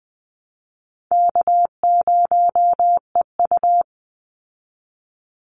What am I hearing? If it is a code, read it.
Morse code, heard as a keyed tone, about 20 wpm, 700 Hz. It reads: K0EU
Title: Call Signs Repeated 3x - 20wpm